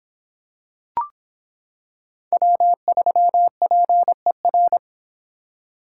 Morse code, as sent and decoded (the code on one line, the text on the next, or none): .-- ...-- .--. .-.
W3PER